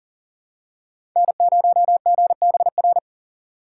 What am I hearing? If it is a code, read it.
Morse code, heard as a keyed tone, about 40 wpm, 700 Hz. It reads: N0GBR
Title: USA Call Signs - 5 Characters - 4x Character Spacing - 40wpm